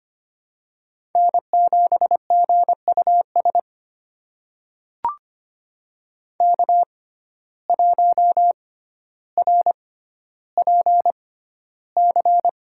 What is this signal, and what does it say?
Morse code, heard as a keyed tone, about 25 wpm, spaced as for 13 wpm, 700 Hz. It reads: N7GUS K1RPC
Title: USA Call Signs - 5 Characters - 6x Character Spacing - 25wpm